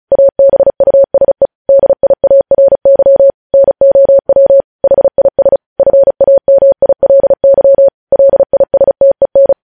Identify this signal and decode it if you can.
Morse code, heard as a keyed tone, about 36 wpm, 550 Hz. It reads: ABUSE DIARY NOW HIS FAMILY LISTEN
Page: News Headlines in Morse Code at 35 WPM